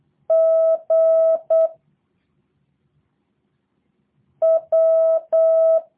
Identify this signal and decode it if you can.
Morse code, heard as a keyed tone, about 8 wpm, 650 Hz. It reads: G W
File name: jh8cbh / 2023cw / review1-4